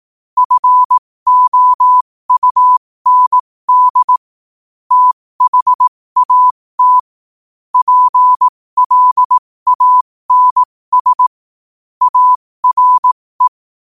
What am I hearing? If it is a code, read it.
Morse code, heard as a keyed tone, about 18 wpm, spaced as for 15 wpm, 1000 Hz. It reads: FOUND THAT PLANS ARE
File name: qotdfiles / 15wpm